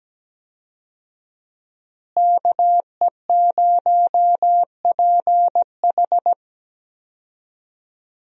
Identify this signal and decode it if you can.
Morse code, heard as a keyed tone, about 17 wpm, 700 Hz. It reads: KE0PH
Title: USA Call Signs - 5 Characters - Repeated 3x - 17wpm